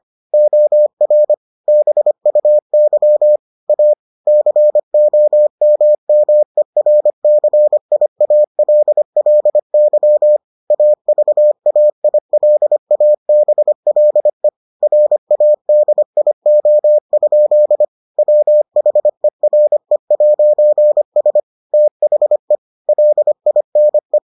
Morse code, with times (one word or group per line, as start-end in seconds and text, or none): 0.3-1.3 OR
1.7-3.4 BUY
3.7-3.9 A
4.3-10.4 COMMERCIALLY
10.7-14.5 AVAILABLE
14.8-17.9 RADIO?
18.2-21.4 WHERE'S
21.7-22.6 THE
22.9-24.2 LINE